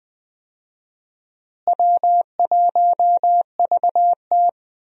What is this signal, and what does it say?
Morse code, heard as a keyed tone, about 20 wpm, 700 Hz. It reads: W1VT
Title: USA Call Signs - 4 Characters - Repeated 3x - 20wpm